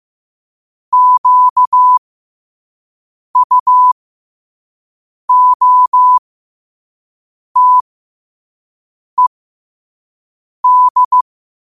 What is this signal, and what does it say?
Morse code, heard as a keyed tone, about 15 wpm, spaced as for 5 wpm, 1000 Hz. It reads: QUOTED
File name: qotdfiles / 5wpm